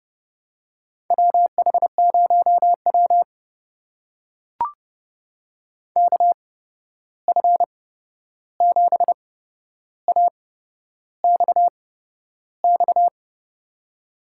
Morse code, heard as a keyed tone, about 30 wpm, spaced as for 11 wpm, 700 Hz. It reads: WH0W KF7AXX
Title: USA Call Signs - 8x Character Spacing - 30wpm